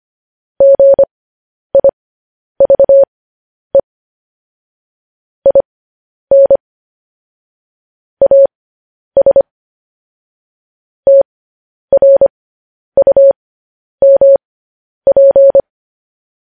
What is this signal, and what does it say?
Morse code, heard as a keyed tone, about 25 wpm, spaced as for 10 wpm, 550 Hz. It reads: GIVE IN AS TRUMP
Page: news-headlines-in-morse-code-at-10-wpm